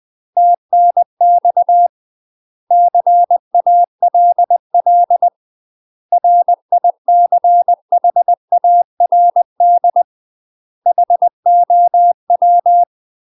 Morse code, heard as a keyed tone, about 20 wpm, 700 Hz. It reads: TNX CALL RICHARD HOW
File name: UK MASTER QSOs TRAINING / 20WPM / UK MASTER QSOs 0030